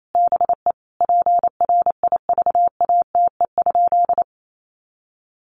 Morse code, heard as a keyed tone, about 28 wpm, 700 Hz. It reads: BE PRIVATE?
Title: Sentences from Top 1000 Words 28wpm